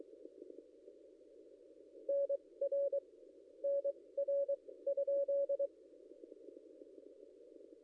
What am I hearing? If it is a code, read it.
Morse code, heard as a keyed tone, about 23 wpm, spaced as for 17 wpm, 560 Hz. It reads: NR NR?
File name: contests / ss-cw-k5tr-op-n5rz -0800